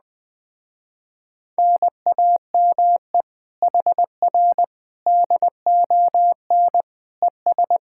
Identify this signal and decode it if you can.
Morse code, heard as a keyed tone, about 20 wpm, 700 Hz. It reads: NAME HR DON ES